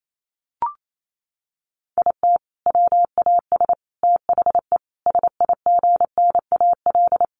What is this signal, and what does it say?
Morse code, heard as a keyed tone, about 28 wpm, 700 Hz. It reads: IT WAS THE SIGNAL
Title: Sentences from Top 1400 Words 28wpm